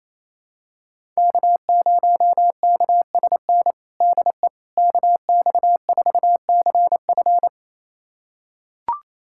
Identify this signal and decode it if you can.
Morse code, heard as a keyed tone, about 28 wpm, 700 Hz. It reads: K0KSN DE KX4CF
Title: USA Callsign DE Callsign 28wpm